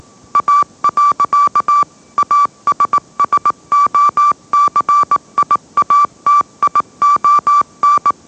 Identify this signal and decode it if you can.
Morse code, heard as a keyed tone, about 18 wpm, 1160 Hz. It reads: A. ASSOCIATION
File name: mrx / mrx082 20wpm